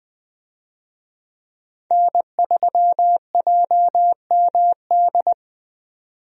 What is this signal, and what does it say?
Morse code, heard as a keyed tone, about 20 wpm, 700 Hz. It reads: N3JMD
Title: USA Call Signs - 5 Characters - 20wpm